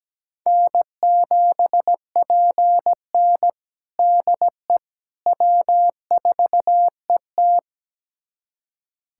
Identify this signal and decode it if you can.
Morse code, heard as a keyed tone, about 17 wpm, 700 Hz. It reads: N7PN DE W4ET
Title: USA Callsign DE Callsign - 4 Characters - 17wpm